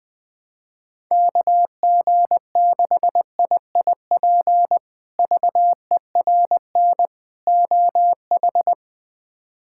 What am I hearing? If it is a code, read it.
Morse code, heard as a keyed tone, about 20 wpm, 700 Hz. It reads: KG6IIP VERN OH
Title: K1USN SST QSO Practice - 20wpm